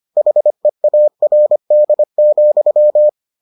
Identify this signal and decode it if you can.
Morse code, heard as a keyed tone, about 25 wpm, 600 Hz. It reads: HEARD,